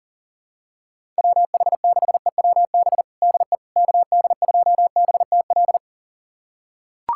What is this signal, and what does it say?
Morse code, heard as a keyed tone, about 40 wpm, 700 Hz. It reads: WH6EWB DE KD2BTL
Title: USA Callsign DE Callsign - 6 Characters - 40wpm